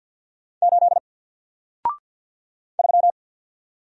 Morse code, heard as a keyed tone, about 49 wpm, 700 Hz. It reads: 8 3
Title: Single Character (Letter:Number:Punctuation:Prosign) 50wpm